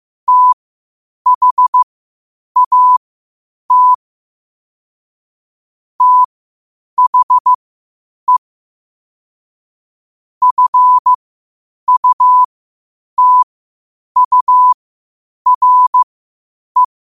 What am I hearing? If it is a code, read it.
Morse code, heard as a keyed tone, about 15 wpm, spaced as for 8 wpm, 1000 Hz. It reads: THAT THE FUTURE
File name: 8wpm